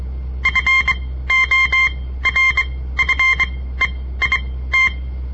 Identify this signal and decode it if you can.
Morse code, heard as a keyed tone, about 23 wpm, spaced as for 15 wpm, 2100 Hz. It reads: FORFEIT
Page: On the air Morse code practice from the Southern Montana Amatuer Radio Association; all code, no read back